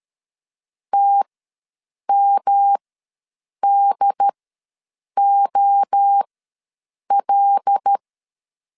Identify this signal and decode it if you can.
Morse code, heard as a keyed tone, about 13 wpm, spaced as for 7 wpm, 800 Hz. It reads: TMDOL
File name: LEZIONI / 5 LBVC / 5 L ws 12wpm 2958